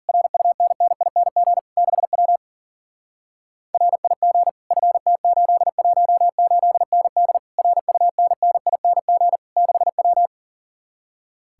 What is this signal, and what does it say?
Morse code, heard as a keyed tone, about 50 wpm, 700 Hz. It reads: RUNNING 6W RIG FT818ND RUNNING 6W